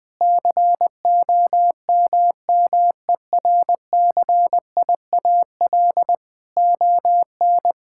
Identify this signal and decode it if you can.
Morse code, heard as a keyed tone, about 20 wpm, 700 Hz. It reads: COMMERCIAL ON
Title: Sentences from Top 1000 Words - Review Part 5 - 20wpm